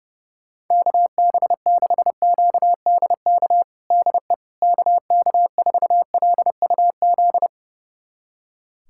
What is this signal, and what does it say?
Morse code, heard as a keyed tone, about 32 wpm, 700 Hz. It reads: KB6QDK DE KK4LUZ
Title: USA Callsign DE Callsign 30wpm